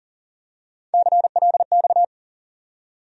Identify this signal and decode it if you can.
Morse code, heard as a keyed tone, about 40 wpm, 700 Hz. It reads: CLX
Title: USA Call Sign Postfixes - 3 Letters 40wpm